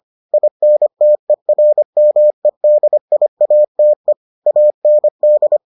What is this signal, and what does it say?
Morse code, heard as a keyed tone, about 25 wpm, 600 Hz. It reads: INTERMEDIATE AND